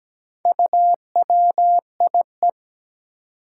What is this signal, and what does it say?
Morse code, heard as a keyed tone, about 17 wpm, 700 Hz. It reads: UWIE
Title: Four Challenging Letter-Number Combinations 17wpm